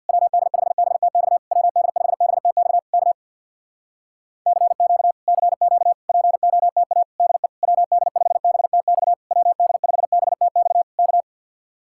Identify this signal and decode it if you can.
Morse code, heard as a keyed tone, about 52 wpm, 700 Hz. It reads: WD5BTX WD5BTX K CQ CQ POTA DE WD5BTX WD5BTX K
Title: Speed-Racing Common QSO Phrases - 50wpm